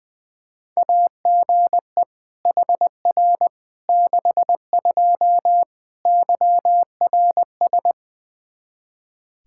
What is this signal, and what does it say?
Morse code, heard as a keyed tone, about 21 wpm, 700 Hz. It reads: AGE HR 62 YRS
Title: Practice QSO - Exchange 3 - More Info - 20wpm